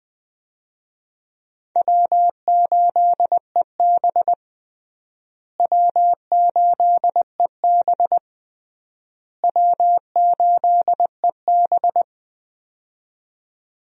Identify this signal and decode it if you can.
Morse code, heard as a keyed tone, about 20 wpm, 700 Hz. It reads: W8EB W8EB W8EB
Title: USA Call Signs - 4 Characters - Repeated 3x - 20wpm